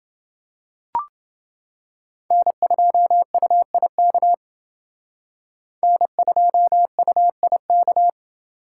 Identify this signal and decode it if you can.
Morse code, heard as a keyed tone, about 28 wpm, 700 Hz. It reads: N2UIK N2UIK